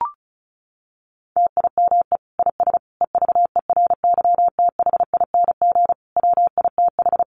Morse code, heard as a keyed tone, about 34 wpm, 700 Hz. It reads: TIME IS EVERYTHING WITH
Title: Sentences from Top 700 Words 35wpm